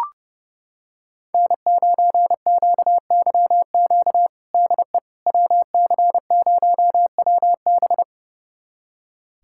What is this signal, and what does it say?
Morse code, heard as a keyed tone, about 30 wpm, 700 Hz. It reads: N9QYQ DE WC0WB
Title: USA Callsign DE Callsign - 5 Characters - 30wpm